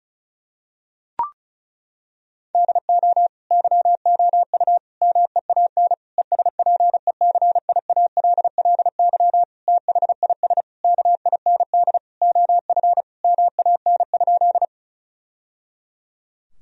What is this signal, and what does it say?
Morse code, heard as a keyed tone, about 34 wpm, 700 Hz. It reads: DO YOU MEAN ESPECIALLY THIS KIND OF MAN?
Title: Sentences from Top 1000 Words - Review Part 4 - 35wpm